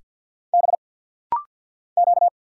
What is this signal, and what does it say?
Morse code, heard as a keyed tone, about 50 wpm, 700 Hz. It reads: B Q
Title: Single Letters - Rapid-Fire 50wpm